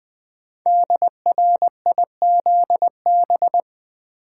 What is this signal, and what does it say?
Morse code, heard as a keyed tone, about 20 wpm, 700 Hz. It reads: DRIZB
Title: Code Groups - 5 Letters - 20wpm